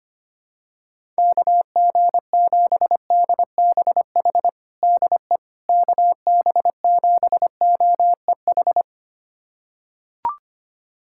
Morse code, heard as a keyed tone, about 25 wpm, 700 Hz. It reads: KG7DBH DE KB7OEH
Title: USA Callsign DE Callsign - 6 Characters - 25wpm